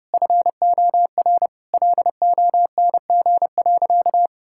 Morse code, heard as a keyed tone, about 30 wpm, 700 Hz. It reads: FOR LONG.
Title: Sentences from Top 1000 Words - Review Part 5 - 30wpm